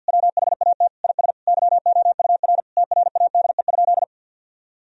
Morse code, heard as a keyed tone, about 50 wpm, 700 Hz. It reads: WHAT IS YOUR TRADE?
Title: Sentences from Top 1000 Words - Review Part 4 - 50wpm